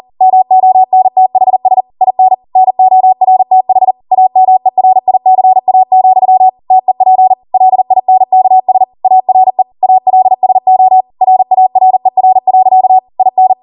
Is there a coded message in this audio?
Yes: Morse code from 0.2 to 13.5 s, heard as a keyed tone, about 40 wpm, 750 Hz. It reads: MONTHS IN NORTH AMERICA, TEP LINKS ARE ALSO RARER. IN